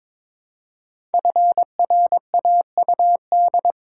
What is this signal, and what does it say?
Morse code, heard as a keyed tone, about 22 wpm, 700 Hz. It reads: FRAUD
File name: 5 Letter Words - First 3 Letters Given - 22wpm